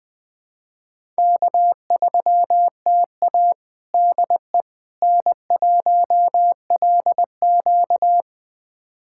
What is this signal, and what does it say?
Morse code, heard as a keyed tone, about 21 wpm, 700 Hz. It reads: K3TA DE N1LQ
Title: USA Callsign DE Callsign - 4 Characters - 20wpm